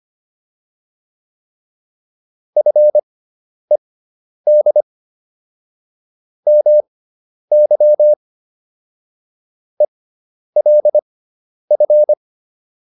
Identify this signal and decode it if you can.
Morse code, heard as a keyed tone, about 25 wpm, spaced as for 10 wpm, 600 Hz. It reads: FED MY ELF